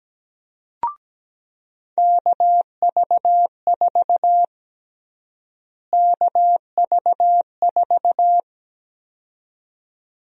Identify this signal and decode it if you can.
Morse code, heard as a keyed tone, about 17 wpm, 700 Hz. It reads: KV4 KV4
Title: USA Call Sign Prefixes - Repeated 2x - 17wpm